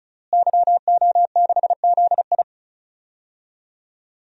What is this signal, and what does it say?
Morse code, heard as a keyed tone, about 34 wpm, 700 Hz. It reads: YO6ZI
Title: DXCC-100 Call Signs - 5 Characters - Repeated 2x - 35wpm